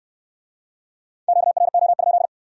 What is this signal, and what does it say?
Morse code, heard as a keyed tone, about 60 wpm, 700 Hz. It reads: QRO?